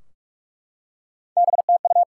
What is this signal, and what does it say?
Morse code, heard as a keyed tone, about 44 wpm, 700 Hz. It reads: BTU
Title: Top 180 QSO Elements 45wpm